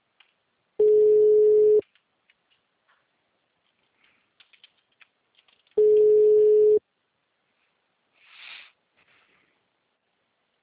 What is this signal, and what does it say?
Ringback tone (Europe, 425 Hz).
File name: out- -2035- - - .116